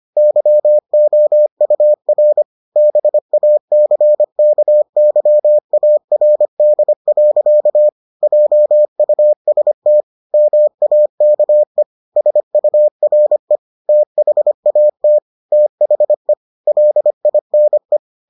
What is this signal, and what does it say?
Morse code, heard as a keyed tone, about 25 wpm, 600 Hz. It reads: YOUR BACKYARD. JUST MAKE SURE THAT THE LINE